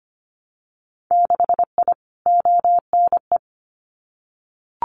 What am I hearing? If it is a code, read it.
Morse code, heard as a keyed tone, about 25 wpm, 700 Hz. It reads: 6I ONE
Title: Winter Field Day Exchanges Encore 25wpm